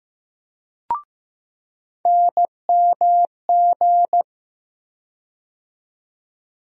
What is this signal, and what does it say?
Morse code, heard as a keyed tone, about 15 wpm, 700 Hz. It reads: NMG
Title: USA Call Sign Postfixes - 3 Letters 15wpm